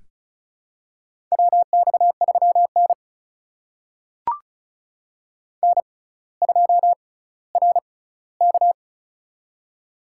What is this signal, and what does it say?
Morse code, heard as a keyed tone, about 36 wpm, spaced as for 18 wpm, 700 Hz. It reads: WX3N N2RK
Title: USA Call Signs - 4 Characters - 6x Character Spacing - 35wpm